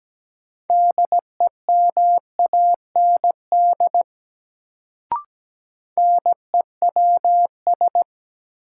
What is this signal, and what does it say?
Morse code, heard as a keyed tone, about 17 wpm, 700 Hz. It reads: DEMAND NEWS
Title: Sentences from Top 800 Words - New Word Drill - 17wpm